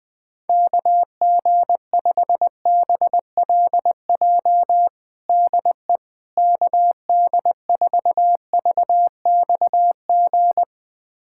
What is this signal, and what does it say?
Morse code, heard as a keyed tone, about 20 wpm, 700 Hz. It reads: KG5BLJ DE KD4VXG
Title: USA Callsign DE Callsign 20wpm